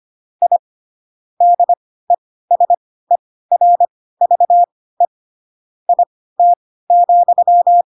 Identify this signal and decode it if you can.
Morse code, heard as a keyed tone, about 25 wpm, spaced as for 16 wpm, 700 Hz. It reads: I DESERVE IT,
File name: Pinocchio / Pinocchio ch3 10 25 ik1hge